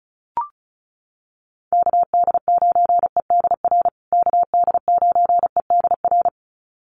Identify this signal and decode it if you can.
Morse code, heard as a keyed tone, about 34 wpm, 700 Hz. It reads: KD9EDR KD9EDR
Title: USA Call Signs Repeated 2x 35wpm